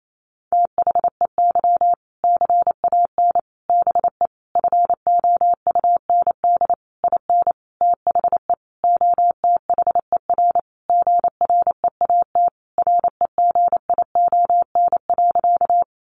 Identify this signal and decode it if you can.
Morse code, heard as a keyed tone, about 28 wpm, 700 Hz. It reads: THEY CAN BE FOUND IN THE OTHER GREAT REGION.
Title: Sentences from Top 1500 Words 28wpm